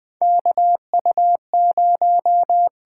Morse code, heard as a keyed tone, about 20 wpm, 700 Hz. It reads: KU0
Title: USA Call Signs - 8x Character Spacing - 20wpm